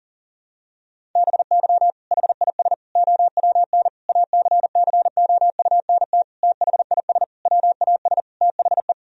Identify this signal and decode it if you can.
Morse code, heard as a keyed tone, about 37 wpm, 700 Hz. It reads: BY HIS OWN ACCOUNT THIS WAS THE
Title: Sentences from Top 1000 Words - Review Part 1 - 40wpm